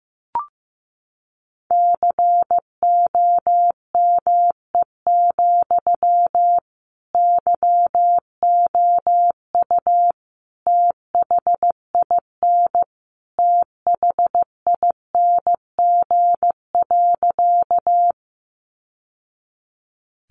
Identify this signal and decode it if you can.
Morse code, heard as a keyed tone, about 15 wpm, 700 Hz. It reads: COME, YOU THIN THING.
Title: Sentences from Top 1200 Words 15wpm